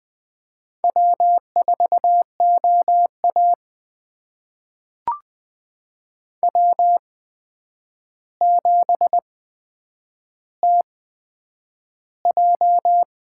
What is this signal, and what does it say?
Morse code, heard as a keyed tone, about 20 wpm, spaced as for 9 wpm, 700 Hz. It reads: W4OA W7TJ
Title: USA Call Signs - 4 Characters - 8x Character Spacing - 20wpm